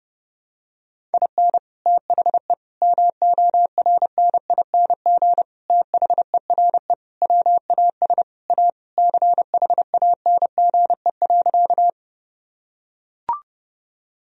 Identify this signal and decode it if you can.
Morse code, heard as a keyed tone, about 30 wpm, 700 Hz. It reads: IN THE MORNING THERE WAS A CHANGE.
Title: Sentences from Top 1000 Words - Review Part 1 - 30wpm